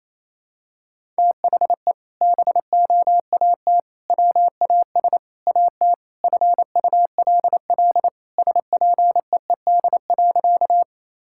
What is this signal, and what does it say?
Morse code, heard as a keyed tone, about 28 wpm, 700 Hz. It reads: THE BOAT WAS AT FULL SPEED.
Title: Sentences from Top 1000 Words - Review Part 5 - 28wpm